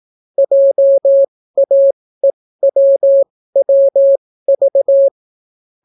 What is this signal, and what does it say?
Morse code, heard as a keyed tone, about 18 wpm, spaced as for 14 wpm, 550 Hz. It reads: JAEWWV